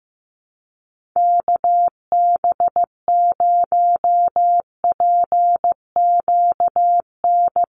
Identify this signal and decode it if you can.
Morse code, heard as a keyed tone, about 15 wpm, 700 Hz. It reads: KB0PQN
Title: USA Call Signs 15wpm